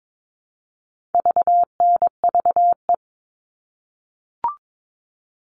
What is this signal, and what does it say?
Morse code, heard as a keyed tone, about 22 wpm, 700 Hz. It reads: VNVE
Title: All Cut Numbers - 4-Digit 22wpm